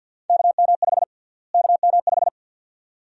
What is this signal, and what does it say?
Morse code, heard as a keyed tone, about 46 wpm, 700 Hz. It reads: KM5 KM5
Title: USA Call Sign Prefixes - Repeated 2x - 50wpm